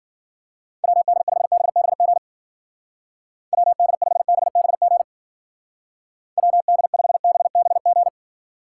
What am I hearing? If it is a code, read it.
Morse code, heard as a keyed tone, about 50 wpm, 700 Hz. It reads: WD5BBG WD5BBG WD5BBG
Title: Speed Racing - USA 2x3 Call Signs 40wpm